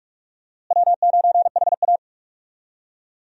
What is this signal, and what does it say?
Morse code, heard as a keyed tone, about 46 wpm, 700 Hz. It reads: W9HA
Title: USA Call Signs - 4 Characters - Repeated 3x - 45wpm